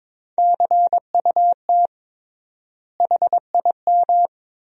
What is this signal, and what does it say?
Morse code, heard as a keyed tone, about 22 wpm, 700 Hz. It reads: CUT HIM
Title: 3x Word-Spacing - Sentences from Top 300 Words - 22wpm